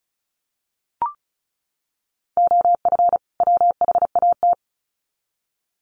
Morse code, heard as a keyed tone, about 34 wpm, 700 Hz. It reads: OF WHAT
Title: Sets of 2 Words 35wpm